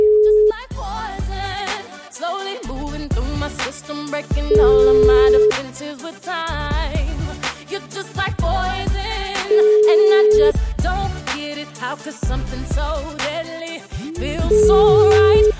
Ringback tone (Europe, 425 Hz).